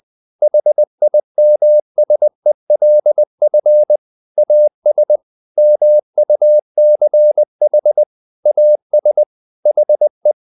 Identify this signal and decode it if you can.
Morse code, heard as a keyed tone, about 20 wpm, 600 Hz. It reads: HIMSELF AS MUCH AS HE
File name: the old man and the sea part 2 20wpm